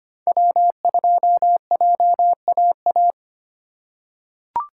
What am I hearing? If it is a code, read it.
Morse code, heard as a keyed tone, about 25 wpm, 700 Hz. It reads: W2JAA
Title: USA Call Signs - 5 Characters - 4x Character Spacing - 25wpm